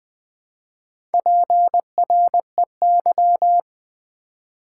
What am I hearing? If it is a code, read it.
Morse code, heard as a keyed tone, about 20 wpm, 700 Hz. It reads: PREY
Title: Build A Word - 1 Letter At A Time - 4-Letter Words - 20wpm